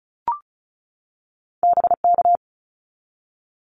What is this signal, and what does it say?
Morse code, heard as a keyed tone, about 36 wpm, 700 Hz. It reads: BK
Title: Abbreviations 35wpm